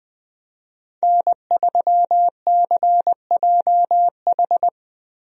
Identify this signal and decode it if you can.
Morse code, heard as a keyed tone, about 20 wpm, 700 Hz. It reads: N3CJH
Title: USA Call Signs - 5 Characters - Repeated 2x - 20wpm